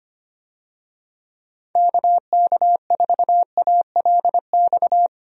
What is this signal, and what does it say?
Morse code, heard as a keyed tone, about 25 wpm, 700 Hz. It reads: KK4ALX
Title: USA Call Signs - 6 Characters - 25wpm